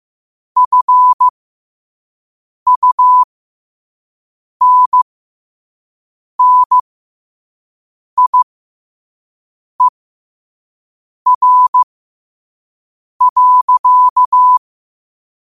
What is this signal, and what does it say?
Morse code, heard as a keyed tone, about 15 wpm, spaced as for 5 wpm, 1000 Hz. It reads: FUNNIER.
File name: qotdfiles / 5wpm